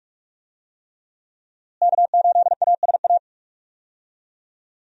Morse code, heard as a keyed tone, about 46 wpm, 700 Hz. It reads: K8ASA
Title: USA Call Signs - 5 Characters - Repeated 3x - 45wpm